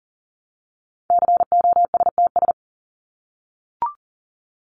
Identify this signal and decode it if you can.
Morse code, heard as a keyed tone, about 37 wpm, 700 Hz. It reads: COSTS
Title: ES:S Suffix - Top 100 Words with Base Words 40wpm